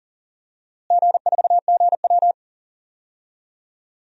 Morse code, heard as a keyed tone, about 40 wpm, 700 Hz. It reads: G4GW
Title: DXCC-100 Call Signs - 4 Characters - Repeated 3x - 40wpm